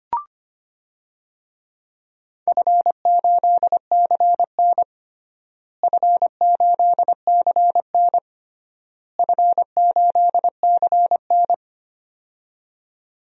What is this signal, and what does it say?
Morse code, heard as a keyed tone, about 25 wpm, 700 Hz. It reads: F8CN F8CN F8CN
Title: DXCC-100 Call Signs - 4 Characters - Repeated 3x - 25wpm